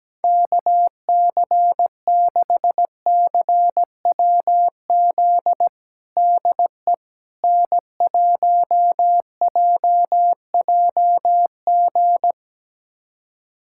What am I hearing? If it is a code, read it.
Morse code, heard as a keyed tone, about 17 wpm, 700 Hz. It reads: KC6CWZ DE N1JJG
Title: USA Callsign DE Callsign 17wpm